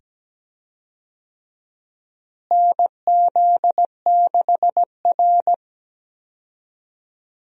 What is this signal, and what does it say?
Morse code, heard as a keyed tone, about 17 wpm, 700 Hz. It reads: NZ6R
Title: USA Call Signs - 4 Characters - Repeated 2x - 17wpm